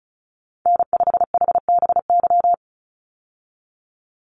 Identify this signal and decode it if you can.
Morse code, heard as a keyed tone, about 34 wpm, 700 Hz. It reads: N5HBY
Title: USA Call Signs 35wpm